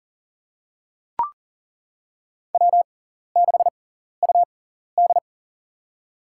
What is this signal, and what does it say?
Morse code, heard as a keyed tone, about 37 wpm, spaced as for 14 wpm, 700 Hz. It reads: W6UD
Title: USA Call Signs - 6x Character Spacing - 40wpm